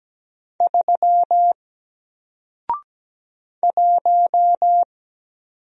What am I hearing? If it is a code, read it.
Morse code, heard as a keyed tone, about 17 wpm, 700 Hz. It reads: 3 1
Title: Single Letter-Number Alt 17wpm